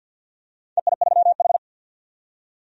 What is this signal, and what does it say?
Morse code, heard as a keyed tone, about 52 wpm, 700 Hz. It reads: EI3H
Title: DXCC-100 Call Signs - 4 Characters - 50wpm